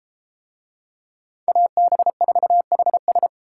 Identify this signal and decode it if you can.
Morse code, heard as a keyed tone, about 32 wpm, 700 Hz. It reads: AB4HS